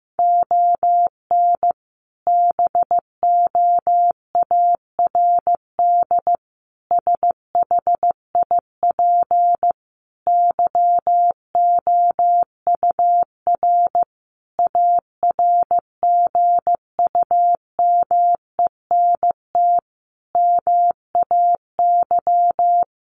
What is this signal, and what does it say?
Morse code, heard as a keyed tone, about 15 wpm, 700 Hz. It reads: ON BOARD SHIP YOUR ARGUMENT MAY
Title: Sentences from Top 1400 Words 15wpm